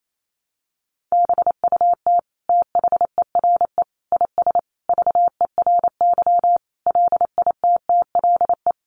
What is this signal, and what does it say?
Morse code, heard as a keyed tone, about 28 wpm, 700 Hz. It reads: BUT THERE IS VERY LITTLE
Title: Sentences from Top 1000 Words 28wpm